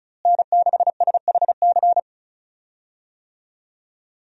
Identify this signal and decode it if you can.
Morse code, heard as a keyed tone, about 35 wpm, 700 Hz. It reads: N6SHC
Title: USA Call Signs - 5 Characters - Repeated 3x - 35wpm